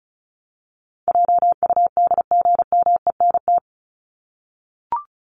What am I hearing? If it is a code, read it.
Morse code, heard as a keyed tone, about 34 wpm, 700 Hz. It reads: JUDGMENT
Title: Top 1751-2000 Words 35wpm